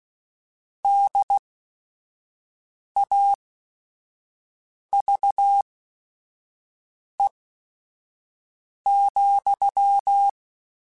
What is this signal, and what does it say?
Morse code, heard as a keyed tone, about 16 wpm, spaced as for 5 wpm, 800 Hz. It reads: DAVE,
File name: QSOs / QSOs-mp3s / kc7sac@5